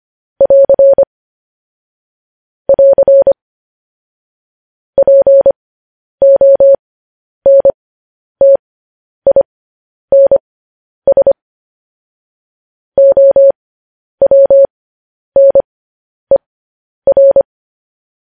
Morse code, heard as a keyed tone, about 25 wpm, spaced as for 10 wpm, 550 Hz. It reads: <AR> <AR> PONTINS OWNER
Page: News Headlines in Morse Code at 10 WPM